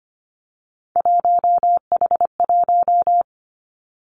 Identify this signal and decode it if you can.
Morse code, heard as a keyed tone, about 25 wpm, 700 Hz. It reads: 1H1
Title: Three Challenging Letter-Number Combinations 25wpm